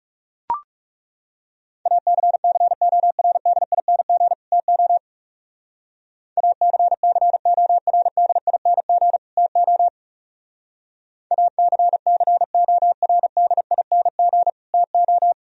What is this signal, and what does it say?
Morse code, heard as a keyed tone, about 36 wpm, 700 Hz. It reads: ACCORDING TO ACCORDING TO ACCORDING TO
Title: Speed-Racing - Sets of 2 Words 30wpm